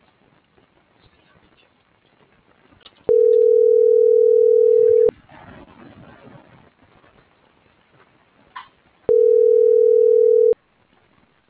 Ringback tone (North America), 440 + 480 Hz.